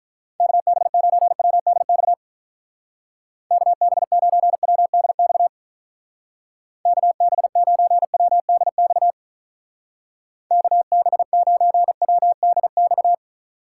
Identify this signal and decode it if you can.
Morse code, heard as a keyed tone, about 46 wpm, 700 Hz. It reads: KB9WDX KB9WDX KB9WDX KB9WDX